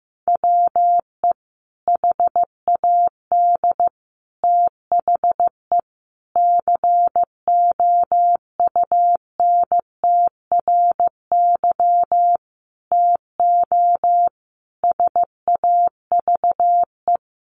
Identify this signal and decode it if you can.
Morse code, heard as a keyed tone, about 15 wpm, 700 Hz. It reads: WE HAD THE COUNTRY TO SAVE
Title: Sentences from Top 1100 Words 15wpm